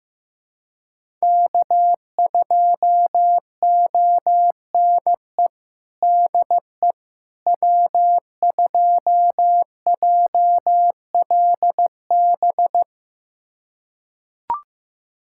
Morse code, heard as a keyed tone, about 15 wpm, 700 Hz. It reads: K2ONE DE W2JLB
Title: USA Callsign DE Callsign - 5 Characters - 15wpm